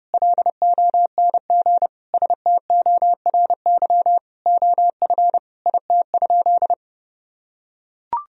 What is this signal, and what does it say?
Morse code, heard as a keyed tone, about 32 wpm, 700 Hz. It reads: LONG STORY OF IT?
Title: Sentences from Top 1000 Words - Review Part 1 - 30wpm